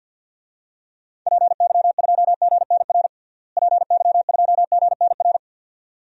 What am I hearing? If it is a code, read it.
Morse code, heard as a keyed tone, about 50 wpm, 700 Hz. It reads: PY2GNR PY2GNR
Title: DXCC-100 Call Signs - 6 Characters - Repeated 2x - 50wpm